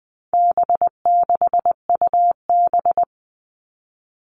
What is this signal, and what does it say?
Morse code, heard as a keyed tone, about 19 wpm, 700 Hz. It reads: B6UB
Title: All Cut Numbers - 4-Digit 20wpm